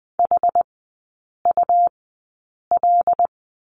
Morse code, heard as a keyed tone, about 20 wpm, spaced as for 8 wpm, 700 Hz. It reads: HUL
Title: 100 Lesson 17e - Letter u - Three Characters 20wpm